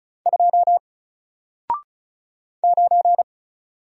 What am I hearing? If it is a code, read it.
Morse code, heard as a keyed tone, about 34 wpm, 700 Hz. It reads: 2 9
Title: Single Letter-Number Alt 35wpm